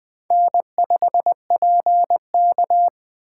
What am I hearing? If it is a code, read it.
Morse code, heard as a keyed tone, about 20 wpm, 700 Hz. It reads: N5PK
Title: Speed Racing - USA 1x2 Call Signs 17wpm